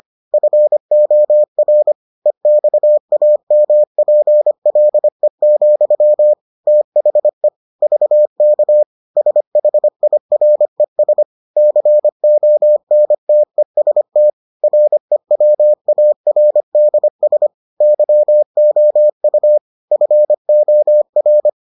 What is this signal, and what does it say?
Morse code, heard as a keyed tone, about 25 wpm, 600 Hz. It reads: FOR EXAMPLE, THE VK SHIRES CONTEST REWARDS YOU FOR